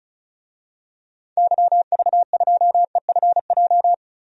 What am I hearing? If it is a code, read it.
Morse code, heard as a keyed tone, about 36 wpm, 700 Hz. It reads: YV2EFJ
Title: DXCC-100 Call Signs - 6 Characters - Repeated 2x - 35wpm